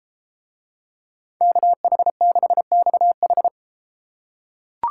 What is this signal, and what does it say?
Morse code, heard as a keyed tone, about 31 wpm, 700 Hz. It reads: KH6XH